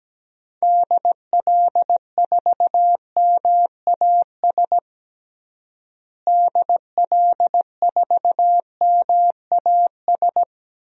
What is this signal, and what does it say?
Morse code, heard as a keyed tone, about 17 wpm, 700 Hz. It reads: DL4MAS DL4MAS
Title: DXCC-100 Call Signs - 6 Characters - Repeated 3x - 17wpm